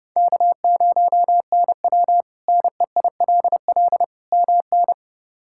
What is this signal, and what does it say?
Morse code, heard as a keyed tone, about 30 wpm, 700 Hz. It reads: K0NW NEILL MN
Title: K1USN SST QSO Practice - 30wpm